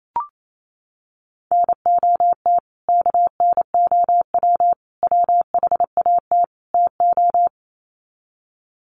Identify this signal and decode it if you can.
Morse code, heard as a keyed tone, about 28 wpm, 700 Hz. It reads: NOT KNOW WHAT TO
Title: Sets of 4 Words 28wpm